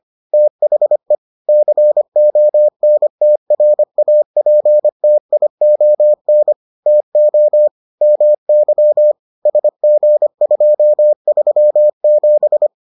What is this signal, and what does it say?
Morse code, heard as a keyed tone, about 25 wpm, 600 Hz. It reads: THE CONTRAPTION TO MY SG237